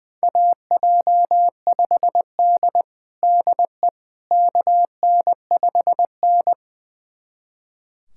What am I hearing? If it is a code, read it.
Morse code, heard as a keyed tone, about 21 wpm, 700 Hz. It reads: AJ5D DE KN5N
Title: USA Callsign DE Callsign - 4 Characters - 20wpm